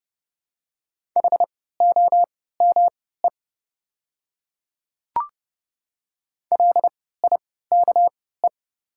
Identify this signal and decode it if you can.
Morse code, heard as a keyed tone, about 30 wpm, spaced as for 17 wpm, 700 Hz. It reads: HOME LIKE
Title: 3x Character Spacing - Top 200 Words - 30wpm